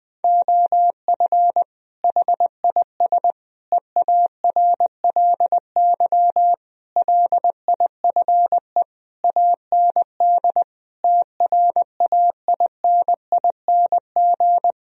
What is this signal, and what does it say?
Morse code, heard as a keyed tone, about 20 wpm, 700 Hz. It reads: OF HIS EARLY LIFE AND TRAINING
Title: Sentences from Top 1000 Words - Review Part 4 - 20wpm